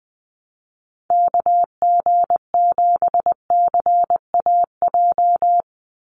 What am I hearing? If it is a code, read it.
Morse code, heard as a keyed tone, about 20 wpm, 700 Hz. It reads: KG7CAJ
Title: USA Call Signs 20wpm